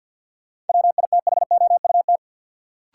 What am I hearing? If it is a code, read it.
Morse code, heard as a keyed tone, about 53 wpm, 700 Hz. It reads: WITHOUT
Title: Sentences from Top 600 Words - New Word Drill - 50wpm